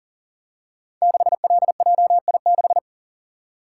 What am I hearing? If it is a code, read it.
Morse code, heard as a keyed tone, about 42 wpm, 690 Hz. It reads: 6LJI6
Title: Code Groups - 5 Letter:Numbers - 40wpm